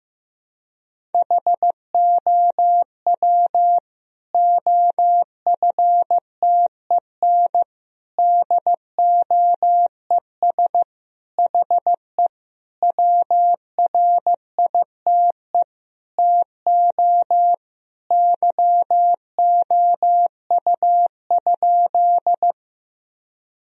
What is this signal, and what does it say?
Morse code, heard as a keyed tone, about 16 wpm, 700 Hz. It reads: HOW OFTEN DOES HE WRITE TO YOU?
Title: Sentences from Top 1000 Words - Review Part 1 - 15wpm